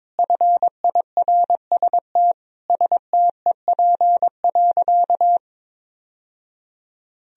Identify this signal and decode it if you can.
Morse code, heard as a keyed tone, about 21 wpm, 700 Hz. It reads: FIRST STEP.
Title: Sentences from Top 1000 Words - Review Part 2 - 22wpm